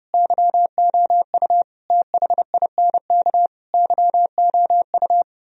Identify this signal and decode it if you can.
Morse code, heard as a keyed tone, about 37 wpm, 700 Hz. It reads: YOU THINK YOU
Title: Speed-Racing - Sets of 4 Words 30wpm